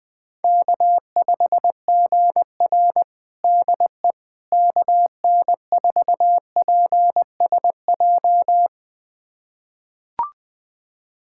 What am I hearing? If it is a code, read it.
Morse code, heard as a keyed tone, about 21 wpm, 700 Hz. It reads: K5GR DE KN4PSJ
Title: USA Callsign DE Callsign 20wpm